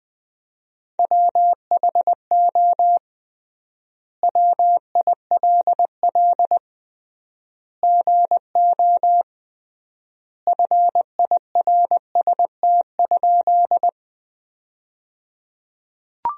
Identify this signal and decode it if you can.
Morse code, heard as a keyed tone, about 21 wpm, 700 Hz. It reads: WHO WILL GO FIRST?
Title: 3x Word-Spacing - Sentences from Top 100 Words - 20wpm